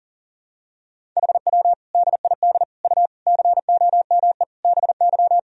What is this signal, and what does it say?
Morse code, heard as a keyed tone, about 37 wpm, 700 Hz. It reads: HW DID U COME BY
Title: Sentences from Top 100 Words and Abbreviations 40wpm